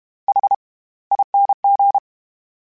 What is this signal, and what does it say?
Morse code, heard as a keyed tone, about 32 wpm, spaced as for 21 wpm, 800 Hz. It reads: HING
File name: Special INGs / Special INGs-mp3s / SlowToFaster ing @30